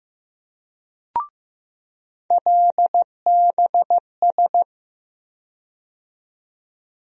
Morse code, heard as a keyed tone, about 15 wpm, 700 Hz. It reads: LBS
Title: USA Call Sign Postfixes - 3 Letters 15wpm